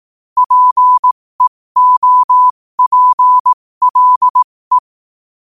Morse code, heard as a keyed tone, about 18 wpm, spaced as for 15 wpm, 1000 Hz. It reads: PEOPLE
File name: qotdfiles / 15wpm